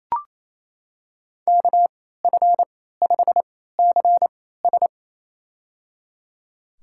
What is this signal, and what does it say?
Morse code, heard as a keyed tone, about 27 wpm, spaced as for 16 wpm, 690 Hz. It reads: KF5CS
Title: USA Call Signs - 3x Character Spacing - 28wpm